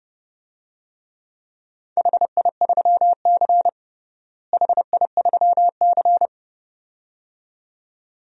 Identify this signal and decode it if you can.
Morse code, heard as a keyed tone, about 30 wpm, 700 Hz. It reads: HI3C HI3C
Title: DXCC-100 Call Signs - 4 Characters - Repeated 2x - 30wpm